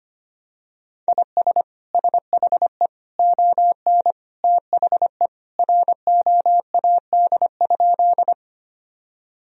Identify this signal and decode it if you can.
Morse code, heard as a keyed tone, about 25 wpm, 700 Hz. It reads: IS SHE ON THE ROAD?
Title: Sentences from Top 1000 Words - Review Part 2 - 25wpm